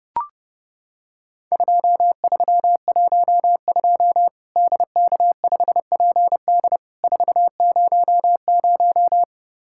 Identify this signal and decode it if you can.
Morse code, heard as a keyed tone, about 30 wpm, 700 Hz. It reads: 2312 DK5PD 400
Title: Worked All Europe DX Contest - QTC Exchange - 30wpm